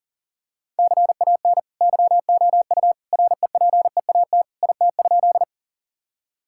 Morse code, heard as a keyed tone, about 37 wpm, 700 Hz. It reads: CAN YOU REPEAT IT?
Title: Sentences from Top 1600 Words 40wpm